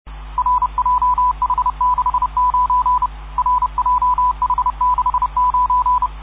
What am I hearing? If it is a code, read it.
Morse code, heard as a keyed tone, about 31 wpm, 1000 Hz. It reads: RJH69 RJH69